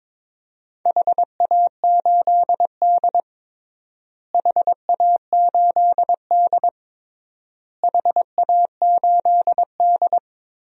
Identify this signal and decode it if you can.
Morse code, heard as a keyed tone, about 22 wpm, 700 Hz. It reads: HA8D HA8D HA8D
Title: DXCC-100 Call Signs - 4 Characters - Repeated 3x - 22wpm